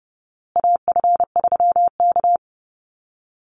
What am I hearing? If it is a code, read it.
Morse code, heard as a keyed tone, about 30 wpm, 700 Hz. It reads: AF3K
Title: Call Signs Repeated 2x - 30wpm